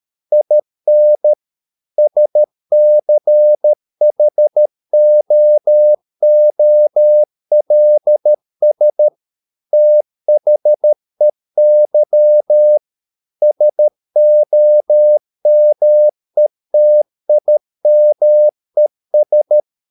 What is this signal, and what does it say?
Morse code, heard as a keyed tone, about 13 wpm, 600 Hz. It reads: IN SCHOOLS THEY SOMETIMES
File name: mp3 files / Lexiphiles - 13wpm